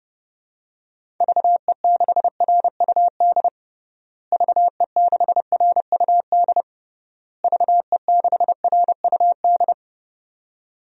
Morse code, heard as a keyed tone, about 30 wpm, 700 Hz. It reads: VE6RUD VE6RUD VE6RUD
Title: DXCC-100 Call Signs - 6 Characters - Repeated 3x - 30wpm